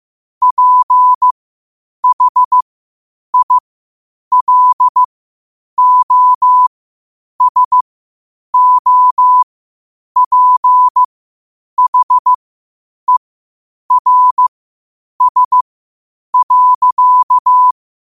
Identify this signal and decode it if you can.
Morse code, heard as a keyed tone, about 15 wpm, spaced as for 8 wpm, 1000 Hz. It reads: PHILOSOPHERS.
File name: qotdfiles / 8wpm